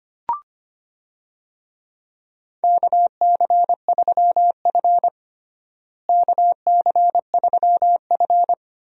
Morse code, heard as a keyed tone, about 25 wpm, 700 Hz. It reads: KC3F KC3F
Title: USA Call Signs - 4 Characters - Repeated 3x - 25wpm